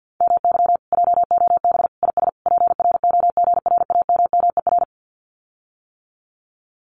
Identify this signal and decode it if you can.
Morse code, heard as a keyed tone, about 50 wpm, 700 Hz. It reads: MY JOB IS PROGRAMMER
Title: Common QSO Phrases Encore 50wpm